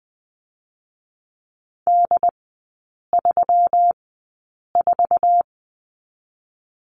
Morse code, heard as a keyed tone, about 20 wpm, spaced as for 8 wpm, 700 Hz. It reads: D34
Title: 127 Lesson 21e - Number 6 - Three Characters 20wpm